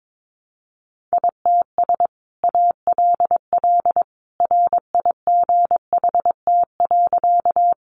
Morse code, heard as a keyed tone, about 22 wpm, 700 Hz. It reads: ITS ALL RIGHT.
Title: Sentences from Top 600 Words 22wpm